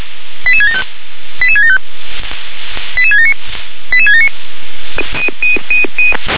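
Five-tone selective call (ZVEI), heard as quick runs of five tones, tones of about 70 ms. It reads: 88675 88675 88679 88679
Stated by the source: ZVEI five-tone selective call 88670 received by SDR.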